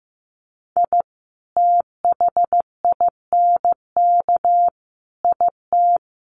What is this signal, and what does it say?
Morse code, heard as a keyed tone, about 15 wpm, 700 Hz. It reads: I THINK IT
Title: Sets of 5 Words 15wpm